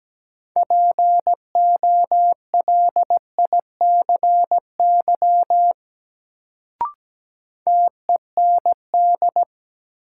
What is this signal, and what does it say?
Morse code, heard as a keyed tone, about 17 wpm, 700 Hz. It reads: POLICY TEND
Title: Sentences from Top 1000 Words - Review Part 5 - New Word Drill - 17wpm